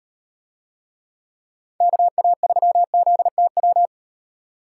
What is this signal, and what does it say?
Morse code, heard as a keyed tone, about 34 wpm, 700 Hz. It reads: KA3ZTW